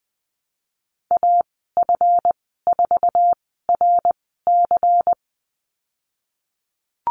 Morse code, heard as a keyed tone, about 19 wpm, spaced as for 14 wpm, 700 Hz. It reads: AF4RC
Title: 147 Lesson 24g - Letter y - Call Signs 20wpm